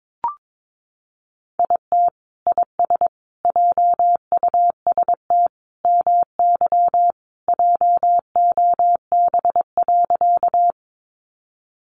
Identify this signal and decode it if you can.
Morse code, heard as a keyed tone, about 22 wpm, 700 Hz. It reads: IT IS JUST MY JOB.
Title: Sentences from Top 600 Words 22wpm